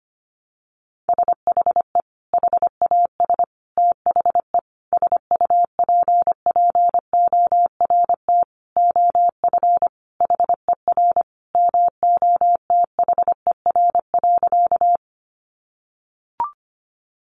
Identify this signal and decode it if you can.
Morse code, heard as a keyed tone, about 25 wpm, 700 Hz. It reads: SHE HAS THE SUPPORT OF HER MOTHER.
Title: Sentences from Top 800 Words 25wpm